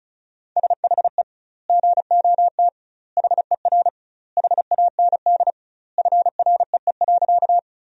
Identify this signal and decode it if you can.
Morse code, heard as a keyed tone, about 34 wpm, 700 Hz. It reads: SHE GOT HER HAND FREE.
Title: 2x Word-Spacing - Sentences from Top 500 Words - 35wpm